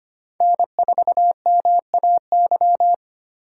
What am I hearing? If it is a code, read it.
Morse code, heard as a keyed tone, about 33 wpm, 700 Hz. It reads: N4MAY